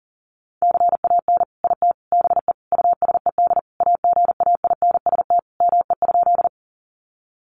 Morse code, heard as a keyed tone, about 40 wpm, 700 Hz. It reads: CAN IT BE USED AGAINST ME?
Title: Sentences from Top 600 Words 40wpm